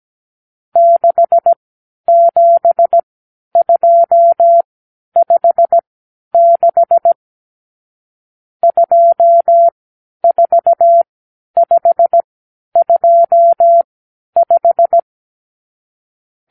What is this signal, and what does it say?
Morse code, heard as a keyed tone, about 17 wpm, spaced as for 10 wpm, 680 Hz. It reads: 67256 24525
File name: k7qo / Lesson 52 (8)